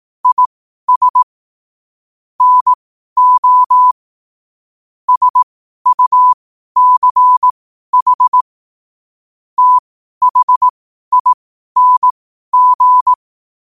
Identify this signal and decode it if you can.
Morse code, heard as a keyed tone, about 18 wpm, spaced as for 12 wpm, 1000 Hz. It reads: IS NO SUCH THING
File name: qotdfiles / 12wpm